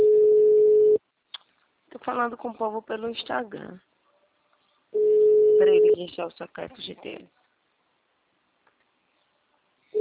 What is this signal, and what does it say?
Ringback tone (Europe, 425 Hz), with speech.